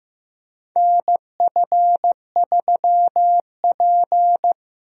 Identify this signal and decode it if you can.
Morse code, heard as a keyed tone, about 15 wpm, 700 Hz. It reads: NF3P
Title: USA Call Signs - 4 Characters - Repeated 2x - 15wpm